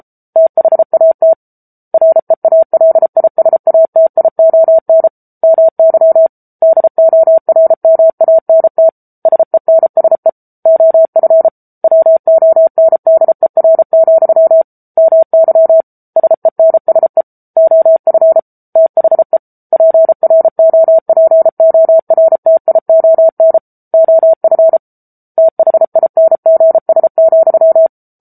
Morse code, heard as a keyed tone, about 34 wpm, 650 Hz. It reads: THAT REALISATION MY DORMANT SENSE OF WONDER, MY SENSE OF THE PROPORTION OF THINGS,
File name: Book Two / 07-Chapter 7 - 33 WPM